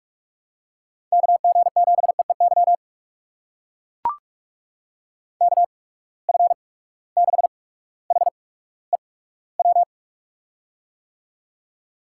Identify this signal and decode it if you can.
Morse code, heard as a keyed tone, about 44 wpm, spaced as for 19 wpm, 700 Hz. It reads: KG7EEY KF6HEW
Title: USA Call Signs - 8x Character Spacing - 45wpm